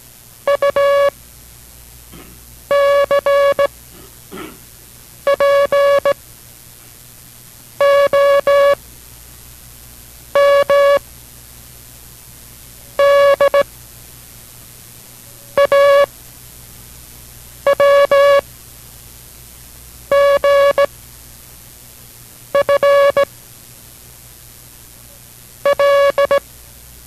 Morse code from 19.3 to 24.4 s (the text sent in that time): G F